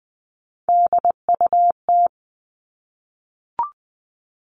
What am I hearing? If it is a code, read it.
Morse code, heard as a keyed tone, about 20 wpm, 700 Hz. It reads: DUT
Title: All Cut Numbers - 3-Digit 20wpm